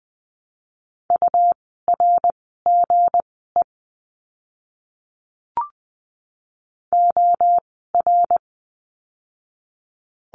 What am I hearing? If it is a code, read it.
Morse code, heard as a keyed tone, about 20 wpm, spaced as for 14 wpm, 700 Hz. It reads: URGE OR
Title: 160 Lesson 26f - Letter g - Words 20wpm